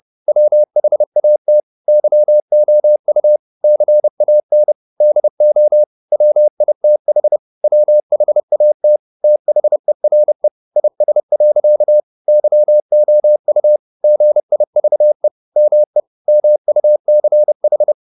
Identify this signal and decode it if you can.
Morse code, heard as a keyed tone, about 30 wpm, 600 Hz. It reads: WHAT YOU CAN DO WITH WHAT THERE IS. YOU GIVE ME MUCH